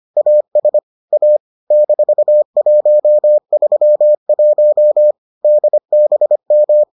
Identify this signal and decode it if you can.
Morse code, heard as a keyed tone, about 25 wpm, 600 Hz. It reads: AS A -131 DBM